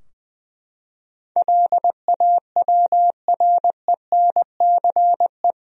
Morse code, heard as a keyed tone, about 21 wpm, 700 Hz. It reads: LAWRENCE
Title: USA Common Names - 20wpm